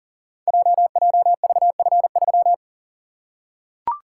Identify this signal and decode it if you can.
Morse code, heard as a keyed tone, about 42 wpm, 690 Hz. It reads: JJVF3
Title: Code Groups - 5 Letter:Numbers - 40wpm